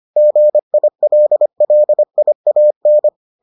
Morse code, heard as a keyed tone, about 25 wpm, 600 Hz. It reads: GILLIAN